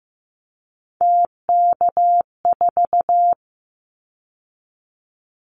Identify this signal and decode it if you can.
Morse code, heard as a keyed tone, about 15 wpm, 700 Hz. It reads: TK4
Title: Callsign Prefixes for Top 100 DXCC - 15wpm